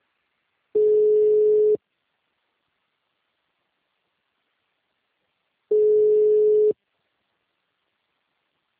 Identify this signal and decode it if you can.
Ringback tone (Europe, 425 Hz).